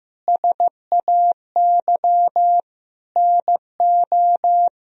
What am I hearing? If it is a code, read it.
Morse code, heard as a keyed tone, about 15 wpm, 700 Hz. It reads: SAY NO
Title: Sentences from Top 1000 Words - Review Part 2 - 15wpm